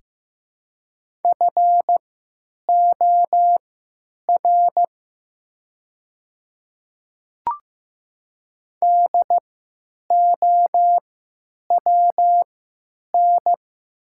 Morse code, heard as a keyed tone, about 15 wpm, spaced as for 8 wpm, 700 Hz. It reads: FOR DOWN
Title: 3x Character Spacing - Top 100 Words - 15wpm